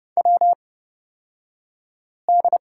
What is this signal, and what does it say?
Morse code, heard as a keyed tone, about 30 wpm, spaced as for 5 wpm, 690 Hz. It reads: WD
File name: Lesson 36d - BT - Two Characters 30wpm